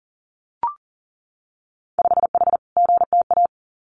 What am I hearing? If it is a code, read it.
Morse code, heard as a keyed tone, about 40 wpm, 700 Hz. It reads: 5H GTA
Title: Winter Field Day Exchanges Encore 40wpm